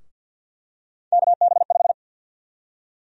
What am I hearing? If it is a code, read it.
Morse code, heard as a keyed tone, about 51 wpm, 700 Hz. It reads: KB5
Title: USA Call Sign Prefixes - 50wpm